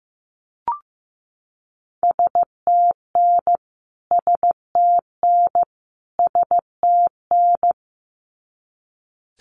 Morse code, heard as a keyed tone, about 15 wpm, 700 Hz. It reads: STN STN STN
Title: Abbreviations Repeated 3x - 15wpm